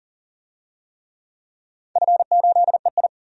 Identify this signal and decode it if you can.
Morse code, heard as a keyed tone, about 40 wpm, 700 Hz. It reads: F8EI
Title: DXCC-100 Call Signs - 4 Characters - 40wpm